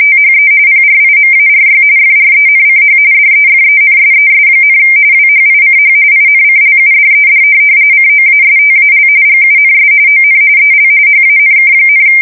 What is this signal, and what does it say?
RTTY, about 45.45 baud. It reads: THE PRIMARY FRENCH DEFENSES, WHICH INCLUDED COASTAL GUNS ON EL HANK,
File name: fsk170 / USS-Massachusetts